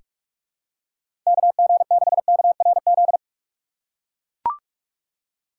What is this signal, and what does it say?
Morse code, heard as a keyed tone, about 44 wpm, 700 Hz. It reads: KG6KRZ
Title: USA Call Signs - 6 Characters - 4x Character Spacing - 45wpm